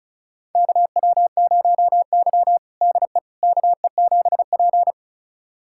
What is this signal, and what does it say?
Morse code, heard as a keyed tone, about 36 wpm, 700 Hz. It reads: KW0Y DE KE7P
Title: USA Callsign DE Callsign - 4 Characters - 35wpm